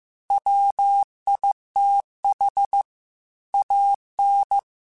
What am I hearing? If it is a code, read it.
Morse code, heard as a keyed tone, about 15 wpm, 800 Hz. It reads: WITH AN
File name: Technical Subjects / Technical Subjects-mp3s / Sw Pwr Sup @13